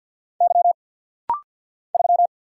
Morse code, heard as a keyed tone, about 50 wpm, 690 Hz. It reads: Y 3
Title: Single Letter-Number Alt - Rapid-Fire 50wpm